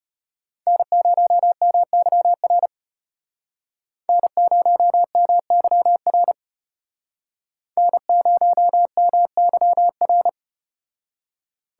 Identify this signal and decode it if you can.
Morse code, heard as a keyed tone, about 34 wpm, 700 Hz. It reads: N0MYR N0MYR N0MYR